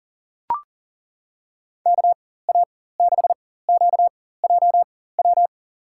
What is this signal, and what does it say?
Morse code, heard as a keyed tone, about 39 wpm, spaced as for 19 wpm, 700 Hz. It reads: KA6QJW
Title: USA Call Signs - 6 Characters - 4x Character Spacing - 40wpm